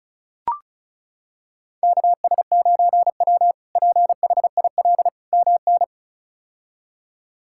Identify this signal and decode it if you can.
Morse code, heard as a keyed tone, about 34 wpm, 700 Hz. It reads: KS9W PHIL MN
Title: K1USN SST QSO Practice - 35wpm